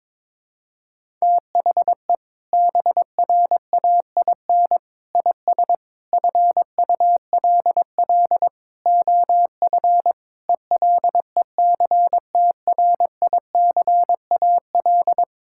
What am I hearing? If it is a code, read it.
Morse code, heard as a keyed tone, about 22 wpm, 700 Hz. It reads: THE BRAIN IS FULL OF ELECTRICAL
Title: Sentences from Top 1000 Words - Review Part 6 - 22wpm